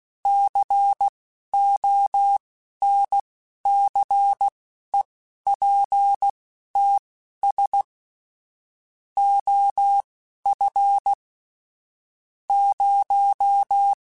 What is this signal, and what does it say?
Morse code, heard as a keyed tone, about 16 wpm, spaced as for 11 wpm, 800 Hz. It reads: CONCEPTS OF 0
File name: Technical Subjects / Technical Subjects-mp3s / Gates Intro @10